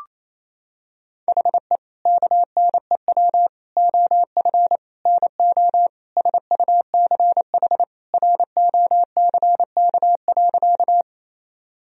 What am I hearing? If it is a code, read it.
Morse code, heard as a keyed tone, about 27 wpm, 700 Hz. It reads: HE KNEW OF NO SUCH ROCK.
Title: Sentences from Top 1000 Words - Review Part 2 - 28wpm